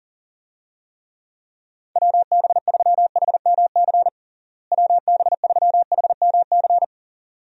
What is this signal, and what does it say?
Morse code, heard as a keyed tone, about 40 wpm, 700 Hz. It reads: WB3HMC WB3HMC
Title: USA Call Signs - 6 Characters - Repeated 2x - 40wpm